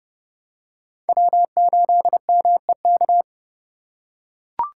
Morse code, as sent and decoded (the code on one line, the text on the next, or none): .-- ---.. -- . -.-
W8MEK